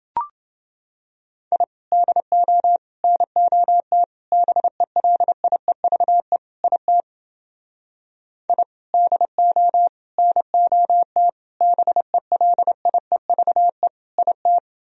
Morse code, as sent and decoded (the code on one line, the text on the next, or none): .. / -.. --- / -. --- - / -... .-.. .. ...- . / .. - / .. / -.. --- / -. --- - / -... .-.. .. ...- . / .. -
I DO NOT BELIEVE IT I DO NOT BELIEVE IT